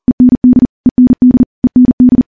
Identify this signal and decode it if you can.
Morse code, heard as a keyed tone, about 40 wpm, 250 Hz. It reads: END END END
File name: morse / morse-training-40WPM